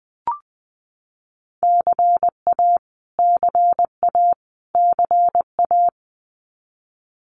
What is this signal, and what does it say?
Morse code, heard as a keyed tone, about 20 wpm, 700 Hz. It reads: CA CA CA
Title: US States Repeated 3x 20wpm